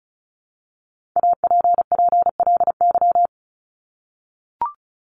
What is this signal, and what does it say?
Morse code, heard as a keyed tone, about 34 wpm, 700 Hz. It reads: APPLY
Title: ED Suffix - Top 100 Words with Base Words 35wpm